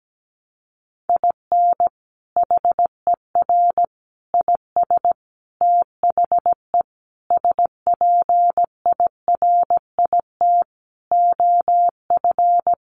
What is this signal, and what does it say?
Morse code, heard as a keyed tone, about 17 wpm, 700 Hz. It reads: IN HER IS THE SPIRIT OF
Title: Sentences from Top 1400 Words 17wpm